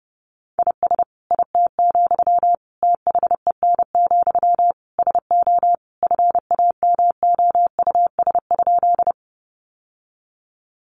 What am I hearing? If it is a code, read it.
Morse code, heard as a keyed tone, about 30 wpm, 700 Hz. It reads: IS IT, THEN, SO FAMOUS?
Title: Sentences from Top 1500 Words 30wpm